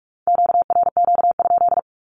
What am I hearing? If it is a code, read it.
Morse code, heard as a keyed tone, about 46 wpm, 700 Hz. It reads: QRQ?
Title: Q-Codes with Spoken Meaning - 45wpm